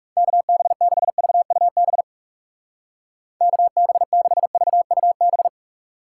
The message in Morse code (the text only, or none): KB6VUB KB6VUB